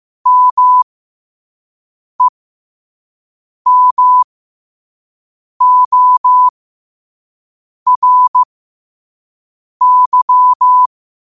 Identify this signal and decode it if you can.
Morse code, heard as a keyed tone, about 15 wpm, spaced as for 5 wpm, 1000 Hz. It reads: MEMORY
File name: 5wpm